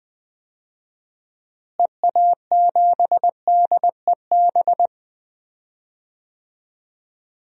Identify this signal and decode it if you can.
Morse code, heard as a keyed tone, about 20 wpm, 700 Hz. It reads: EA7DEB
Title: DXCC-100 Call Signs - 6 Characters - Repeated 2x - 20wpm